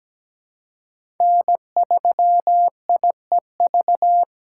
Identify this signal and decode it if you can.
Morse code, heard as a keyed tone, about 17 wpm, 700 Hz. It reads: N3IEV
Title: USA Call Signs - 5 Characters - 2x Character Spacing - 17wpm